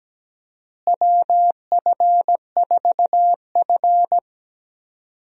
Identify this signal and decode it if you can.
Morse code, heard as a keyed tone, about 17 wpm, 700 Hz. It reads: WF4F
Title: USA Call Signs - 4 Characters - Repeated 3x - 17wpm